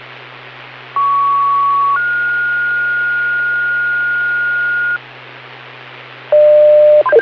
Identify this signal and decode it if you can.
Two-tone page, heard as a one-second tone, then a three-second tone, about 1120 then 1430 Hz, with noise. It is heard from 1.0 to 5.0 s.